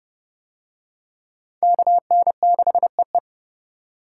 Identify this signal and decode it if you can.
Morse code, heard as a keyed tone, about 25 wpm, 700 Hz. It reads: KN6EE